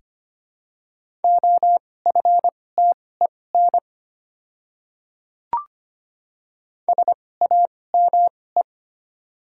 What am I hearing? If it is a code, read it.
Morse code, heard as a keyed tone, about 25 wpm, spaced as for 18 wpm, 700 Hz. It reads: OFTEN SAME
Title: 2x Character Spacing - Top 300 Words - 25wpm